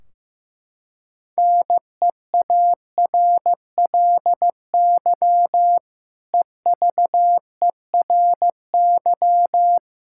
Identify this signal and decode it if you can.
Morse code, heard as a keyed tone, about 15 wpm, 700 Hz. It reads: NEARLY EVERY
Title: Sentences from Top 1600 Words 15wpm